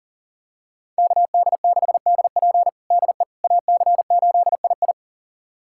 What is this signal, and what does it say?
Morse code, heard as a keyed tone, about 40 wpm, 700 Hz. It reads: KD6DP DE AC8II
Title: USA Callsign DE Callsign - 5 Characters - 40wpm